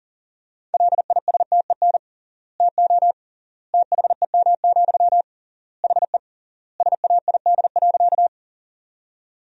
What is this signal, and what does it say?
Morse code, heard as a keyed tone, about 40 wpm, 700 Hz. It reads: LISTEN TO THEM, HE SAID.
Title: 3x Word-Spacing - Sentences from Top 400 Words - 40wpm